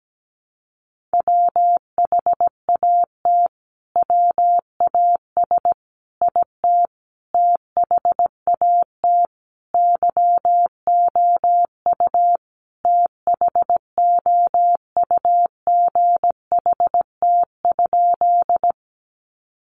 Morse code, heard as a keyed tone, about 17 wpm, 700 Hz. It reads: WHAT WAS IT THAT YOU THOUGHT?
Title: Sentences from Top 300 Words 17wpm